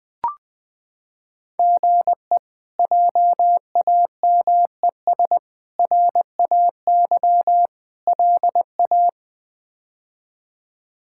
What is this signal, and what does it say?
Morse code, heard as a keyed tone, about 20 wpm, 700 Hz. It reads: GE JAMES RAY LA
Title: K1USN SST QSO Practice - 20wpm